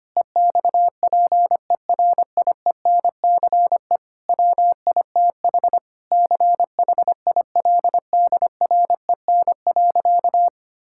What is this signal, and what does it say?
Morse code, heard as a keyed tone, about 25 wpm, 700 Hz. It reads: EXPERIENCE WITH CHILDREN.
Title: Sentences from Top 1000 Words - Review Part 5 - 25wpm